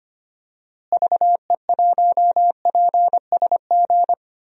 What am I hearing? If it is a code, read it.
Morse code, heard as a keyed tone, about 25 wpm, 700 Hz. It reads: VE1PSG
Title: DXCC-100 Call Signs - 6 Characters - Repeated 3x - 25wpm